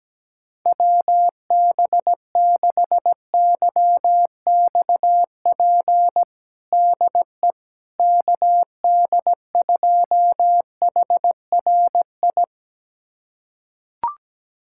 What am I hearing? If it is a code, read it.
Morse code, heard as a keyed tone, about 17 wpm, 700 Hz. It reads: WB6YXP DE KD2HRI
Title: USA Callsign DE Callsign 17wpm